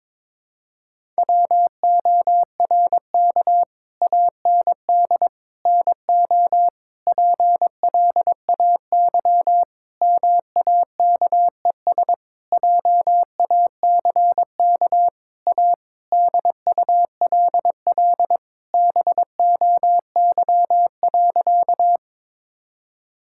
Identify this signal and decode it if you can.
Morse code, heard as a keyed tone, about 22 wpm, 700 Hz. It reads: WORK AND NO PLAY MAKES JACK A DULL BOY.
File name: Sentences with 7-Word Phrases - 22wpm